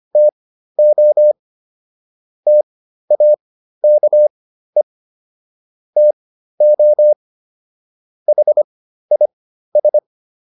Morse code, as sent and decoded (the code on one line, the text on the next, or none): - --- / - .- -.- . / - --- / .... .. ...
TO TAKE TO HIS